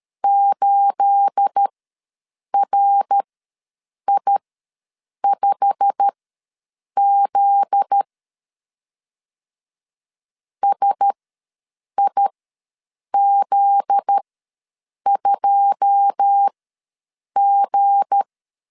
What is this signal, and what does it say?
Morse code, heard as a keyed tone, about 13 wpm, spaced as for 7 wpm, 800 Hz. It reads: 8RI5Z SIZ2G
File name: LEZIONI / 9 28 ws 12wpm 3354